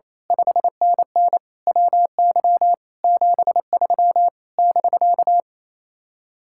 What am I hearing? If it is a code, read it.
Morse code, heard as a keyed tone, about 26 wpm, 700 Hz. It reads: 5NN WY 73 <BK>
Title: Speed-Racing Common QSO Phrases - 28wpm